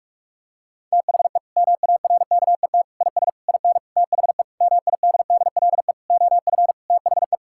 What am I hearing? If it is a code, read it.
Morse code, heard as a keyed tone, about 41 wpm, 700 Hz. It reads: THE MARKET IS IN THE MIDDLE OF THE
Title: Sentences from Top 1000 Words - Review Part 6 - 45wpm